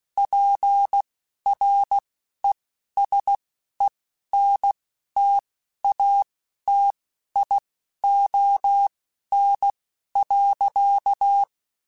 Morse code, heard as a keyed tone, about 16 wpm, spaced as for 11 wpm, 800 Hz. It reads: PRESENTATION.
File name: Quotes Spaced / Quotes Spaced-mp3s / Persperation sp @15